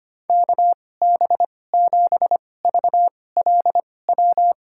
Morse code, heard as a keyed tone, about 25 wpm, spaced as for 18 wpm, 700 Hz. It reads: KB7VLW
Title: USA Call Signs - 6 Characters - 2x Character Spacing - 25wpm